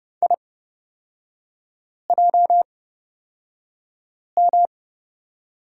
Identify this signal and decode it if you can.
Morse code, heard as a keyed tone, about 30 wpm, spaced as for 5 wpm, 700 Hz. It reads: IJM
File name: Lesson 38e - Letter q - Three Characters 30wpm